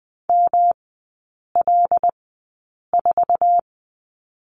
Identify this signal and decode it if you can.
Morse code, heard as a keyed tone, about 20 wpm, spaced as for 8 wpm, 700 Hz. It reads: ML4
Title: 107 Lesson 18e - Letter m - Three Characters 20wpm